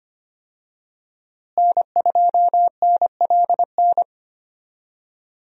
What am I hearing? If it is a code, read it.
Morse code, heard as a keyed tone, about 25 wpm, 700 Hz. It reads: N2NLN
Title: USA Call Signs - 5 Characters - Repeated 2x - 25wpm